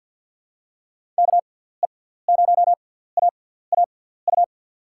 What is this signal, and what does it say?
Morse code, heard as a keyed tone, about 47 wpm, spaced as for 17 wpm, 700 Hz. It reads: KE0AAU
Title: USA Call Signs - 6 Characters - 6x Character Spacing - 50wpm